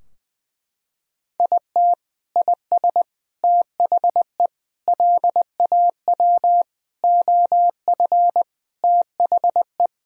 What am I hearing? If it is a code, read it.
Morse code, heard as a keyed tone, about 20 wpm, 700 Hz. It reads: IT IS THE LAW OF THE
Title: Sentences from Top 1000 Words - Review Part 3 - 20wpm